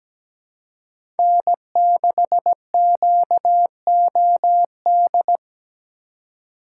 Morse code, heard as a keyed tone, about 17 wpm, 700 Hz. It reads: N6QOD
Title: USA Call Signs - 5 Characters - 2x Character Spacing - 17wpm